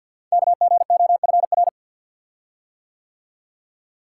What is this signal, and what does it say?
Morse code, heard as a keyed tone, about 50 wpm, 690 Hz. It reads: KGOFR
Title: Code Groups - 5 Letters - 50wpm